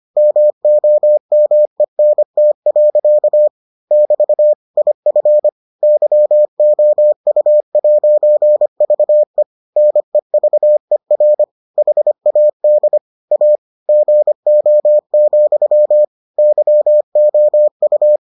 Morse code, heard as a keyed tone, about 25 wpm, 600 Hz. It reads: MOMENT. <BT> IF YOU'VE NEVER HAD A GO, YOU